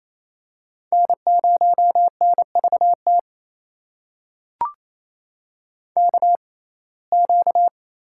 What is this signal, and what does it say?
Morse code, heard as a keyed tone, about 28 wpm, spaced as for 20 wpm, 700 Hz. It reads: N0NVT KQ
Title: USA Call Signs - 5 Characters - 6x Character Spacing - 28wpm